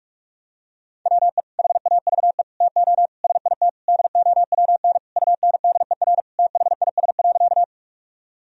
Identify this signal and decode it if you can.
Morse code, heard as a keyed tone, about 44 wpm, 700 Hz. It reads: WE HAVE TO SIT DOWN UNDER THIS.
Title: Sentences from Top 1000 Words - Review Part 1 - 45wpm